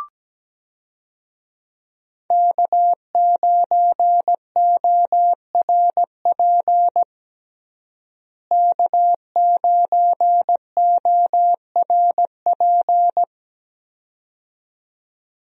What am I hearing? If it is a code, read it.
Morse code, heard as a keyed tone, about 17 wpm, 700 Hz. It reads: K9ORP K9ORP
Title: USA Call Signs - 5 Characters - Repeated 2x - 17wpm